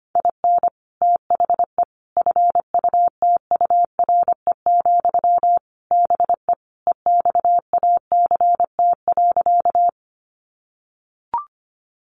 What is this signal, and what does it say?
Morse code, heard as a keyed tone, about 25 wpm, 700 Hz. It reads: IN THE FUTURE, BE EXACT.
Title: Sentences from Top 1300 Words 25wpm